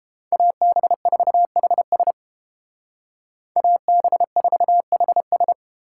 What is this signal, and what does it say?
Morse code, heard as a keyed tone, about 31 wpm, 700 Hz. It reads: AB4HS AB4HS